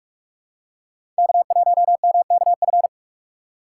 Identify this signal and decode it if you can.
Morse code, heard as a keyed tone, about 46 wpm, 700 Hz. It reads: K1MKF
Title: USA Call Signs - 5 Characters - 4x Character Spacing - 45wpm